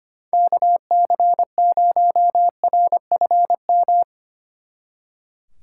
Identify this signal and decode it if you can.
Morse code, heard as a keyed tone, about 30 wpm, 700 Hz. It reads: KC0RFM
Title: Speed Racing - USA 2x3 Call Signs 25wpm